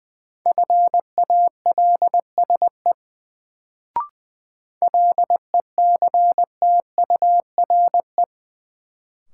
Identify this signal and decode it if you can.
Morse code, heard as a keyed tone, about 20 wpm, 700 Hz. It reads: FALSE LECTURE
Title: Sentences from Top 1600 Words - New Word Drill - 20wpm